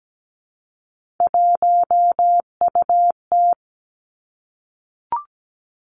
Morse code, heard as a keyed tone, about 17 wpm, 700 Hz. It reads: 1UT
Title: Three Challenging Letter-Number Combinations 17wpm